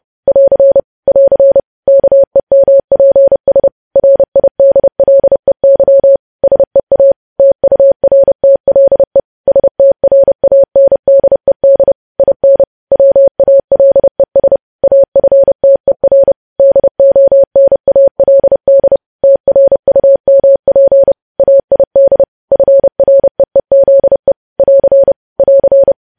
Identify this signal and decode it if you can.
Morse code, heard as a keyed tone, about 30 wpm, 550 Hz. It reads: <AR> <AR> KEMPS RIDLEY SEA TURTLE STRANDED IN WALES AFTER DONALD TRUMP AID FREEZE <AR> <AR>
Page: news headlines in morse code at 30 wpm